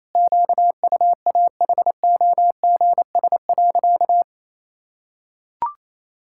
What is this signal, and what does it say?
Morse code, heard as a keyed tone, about 27 wpm, 700 Hz. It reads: QUAHOGS.
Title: Pangram Challenge v2 - 28wpm